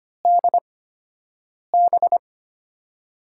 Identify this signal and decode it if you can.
Morse code, heard as a keyed tone, about 25 wpm, spaced as for 7 wpm, 700 Hz. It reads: DB
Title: USA Call Signs - 5 Characters - 8x Character Spacing - 25wpm